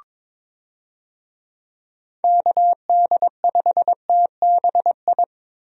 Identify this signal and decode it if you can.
Morse code, heard as a keyed tone, about 21 wpm, 700 Hz. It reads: KD5TBI
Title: USA Call Signs - 6 Characters - Repeated 2x - 22wpm